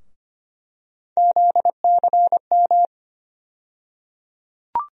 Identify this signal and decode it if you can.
Morse code, heard as a keyed tone, about 25 wpm, 700 Hz. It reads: ZCM
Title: USA Call Sign Postfixes - 3 Letters 25wpm